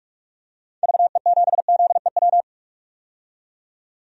Morse code, heard as a keyed tone, about 43 wpm, 700 Hz. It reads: VE7ZEW
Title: DXCC-100 Call Signs - 6 Characters - 45wpm